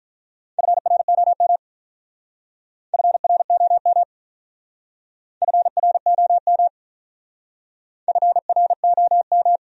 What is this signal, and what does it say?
Morse code, heard as a keyed tone, about 47 wpm, 700 Hz. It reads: FROM FROM FROM FROM